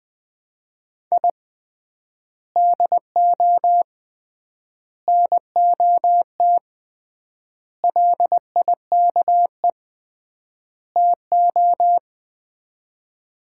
Morse code, heard as a keyed tone, about 20 wpm, 700 Hz. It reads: I DO NOT LIKE TO
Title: 3x Word-Spacing - Sets of 5 Words - 20wpm